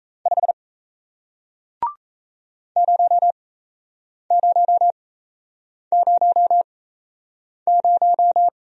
Morse code, heard as a keyed tone, about 38 wpm, spaced as for 8 wpm, 700 Hz. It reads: 5 0000